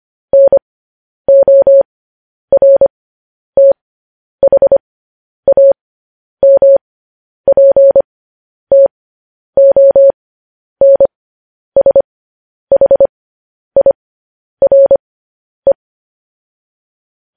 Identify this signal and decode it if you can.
Morse code, heard as a keyed tone, about 25 wpm, spaced as for 10 wpm, 550 Hz. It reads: NORTHAMPTONSHIRE
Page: news headlines in morse code at 10 wpm